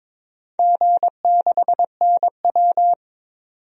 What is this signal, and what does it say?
Morse code, heard as a keyed tone, about 22 wpm, 700 Hz. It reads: G6NW
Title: DXCC-100 Call Signs - 4 Characters - 22wpm